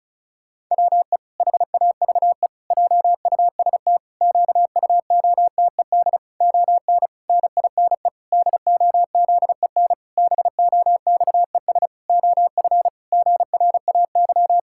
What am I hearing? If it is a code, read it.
Morse code, heard as a keyed tone, about 36 wpm, 700 Hz. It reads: WE HAVE JUST QUOTED ON NINE DOZEN BOXES OF GRAY
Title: Pangram Challenge v2 - 35wpm